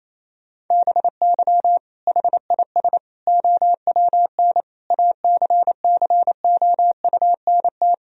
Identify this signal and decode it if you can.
Morse code, heard as a keyed tone, about 28 wpm, 700 Hz. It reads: BY HIS OWN ACCOUNT
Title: Sentences from Top 1000 Words - Review Part 1 - 28wpm